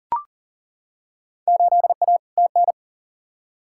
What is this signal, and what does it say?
Morse code, heard as a keyed tone, about 40 wpm, 700 Hz. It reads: 8A TN
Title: Field Day Exchanges Encore - 40wpm